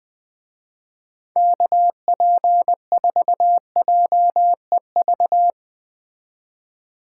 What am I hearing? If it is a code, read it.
Morse code, heard as a keyed tone, about 20 wpm, 700 Hz. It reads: KP4JEV